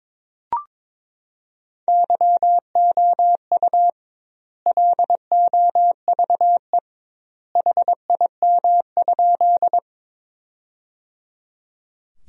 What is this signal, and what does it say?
Morse code, heard as a keyed tone, about 22 wpm, 700 Hz. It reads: YOU LOVE HIM?
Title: 2x Word-Spacing - Sentences from Top 400 Words - 22wpm